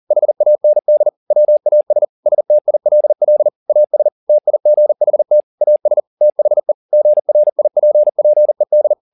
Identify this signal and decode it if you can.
Morse code, heard as a keyed tone, about 40 wpm, 600 Hz. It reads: HAND WAS STILL AS TIGHT AS THE GRIPPED